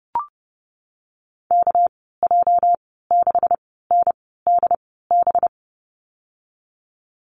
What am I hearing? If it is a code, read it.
Morse code, heard as a keyed tone, about 30 wpm, spaced as for 17 wpm, 700 Hz. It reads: KJ6NDB
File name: Lesson 37g - Letter x - Call Signs 30wpm